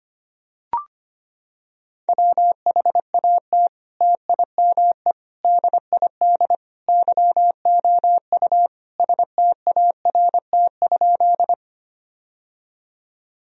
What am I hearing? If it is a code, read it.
Morse code, heard as a keyed tone, about 25 wpm, 700 Hz. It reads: WHAT TIME DID YOU START?
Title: Sentences from Top 1000 Words - Review Part 1 - 25wpm